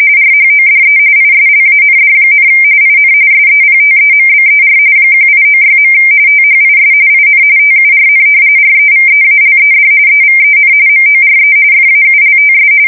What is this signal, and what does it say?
RTTY, about 45.45 baud. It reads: RHODE ISLAND, IN SEPTEMBER 1896. THIS IN TURN REVEALED EVEN MORE FAULTS
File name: fsk170 / battleships